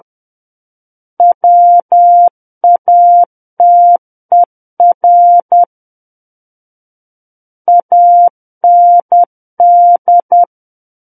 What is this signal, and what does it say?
Morse code, heard as a keyed tone, about 10 wpm, 700 Hz. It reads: WATER AND
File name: Book One / 01-Chapter 1 - 10 WPM